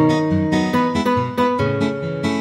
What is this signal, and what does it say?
There is no code.